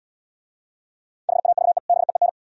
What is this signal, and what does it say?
Morse code, heard as a keyed tone, about 72 wpm, 700 Hz. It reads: HAVE BEEN